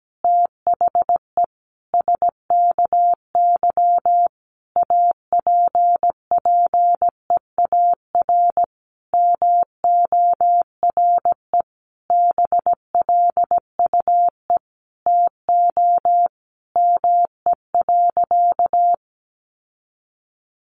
Morse code, heard as a keyed tone, about 17 wpm, 700 Hz. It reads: THE SKY APPEAR MORE BLUE TO ME.
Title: Sentences from Top 1200 Words 17wpm